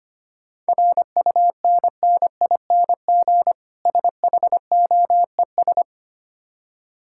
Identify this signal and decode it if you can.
Morse code, heard as a keyed tone, about 25 wpm, 700 Hz. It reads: RUNNING SHOES
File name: practice / 25wpm / Difficult Words with Hints - 25wpm